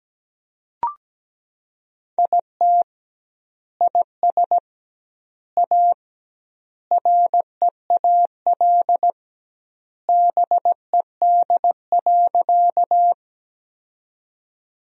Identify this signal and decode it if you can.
Morse code, heard as a keyed tone, about 17 wpm, 700 Hz. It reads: IT IS A REAL BED.
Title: 2x Word-Spacing - Sentences from Top 500 Words - 17wpm